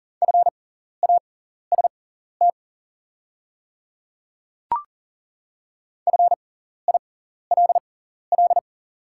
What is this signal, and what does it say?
Morse code, heard as a keyed tone, about 40 wpm, spaced as for 14 wpm, 690 Hz. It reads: FAST FILL
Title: 6x Character Spacing - Top 500 Words - 40wpm